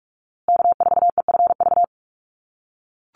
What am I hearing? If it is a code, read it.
Morse code, heard as a keyed tone, about 44 wpm, 700 Hz. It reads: K4EFV
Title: USA Call Signs Repeated 3x 45wpm